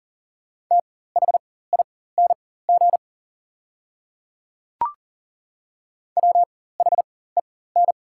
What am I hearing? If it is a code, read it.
Morse code, heard as a keyed tone, about 42 wpm, spaced as for 19 wpm, 690 Hz. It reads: THING WHEN
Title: 4x Character Spacing - Top 300 Words - 40wpm